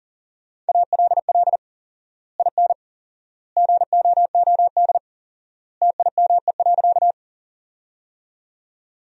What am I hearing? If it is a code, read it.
Morse code, heard as a keyed tone, about 40 wpm, 700 Hz. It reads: ALL IN GOOD TIME.
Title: 4x Word-Spacing - Sentences from Top 200 Words - 40wpm